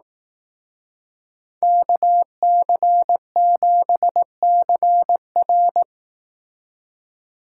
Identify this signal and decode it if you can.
Morse code, heard as a keyed tone, about 19 wpm, 700 Hz. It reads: KC7CR
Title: Speed Racing - USA Call Signs 15wpm